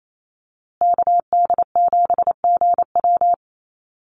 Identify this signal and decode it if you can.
Morse code, heard as a keyed tone, about 28 wpm, 700 Hz. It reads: KD7GW
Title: USA Call Signs Repeated 2x 28wpm